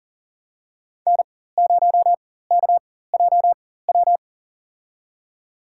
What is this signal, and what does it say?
Morse code, heard as a keyed tone, about 40 wpm, spaced as for 19 wpm, 700 Hz. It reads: N0KJW
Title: USA Call Signs - 4x Character Spacing - 40wpm